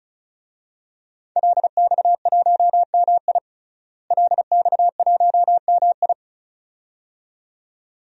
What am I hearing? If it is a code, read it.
Morse code, heard as a keyed tone, about 36 wpm, 700 Hz. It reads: LX1MI LX1MI
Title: DXCC-100 Call Signs - 5 Characters - Repeated 2x - 35wpm